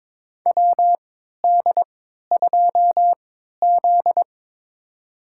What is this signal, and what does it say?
Morse code, heard as a keyed tone, about 23 wpm, spaced as for 13 wpm, 700 Hz. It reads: WD2Z
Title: USA Call Signs - 4 Characters - 3x Character Spacing - 22wpm